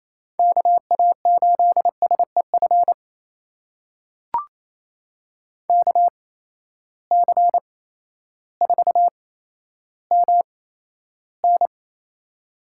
Morse code, heard as a keyed tone, about 28 wpm, spaced as for 13 wpm, 700 Hz. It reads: KA8SEF KC4MN
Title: USA Call Signs - 8x Character Spacing - 28wpm